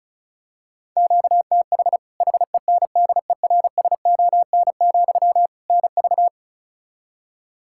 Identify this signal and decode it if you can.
Morse code, heard as a keyed tone, about 34 wpm, 700 Hz. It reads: QTH HENDERSON, NV
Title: Practice QSO - Exchange 1 - Beginning - 35wpm